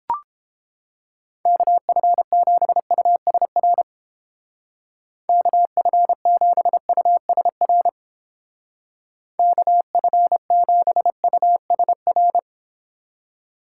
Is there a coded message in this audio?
Yes: Morse code from 1.5 to 12.4 s, heard as a keyed tone, about 31 wpm, 700 Hz. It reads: KF7USR KF7USR KF7USR